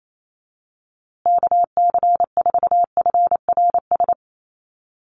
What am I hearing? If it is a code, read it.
Morse code, heard as a keyed tone, about 28 wpm, 700 Hz. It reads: KC4FRS
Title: USA Call Signs Repeated 3x 28wpm